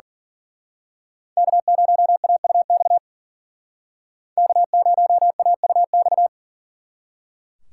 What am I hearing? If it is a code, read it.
Morse code, heard as a keyed tone, about 41 wpm, 700 Hz. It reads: K0AUX K0AUX
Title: Speed Racing - USA 1x3 Call Signs 40wpm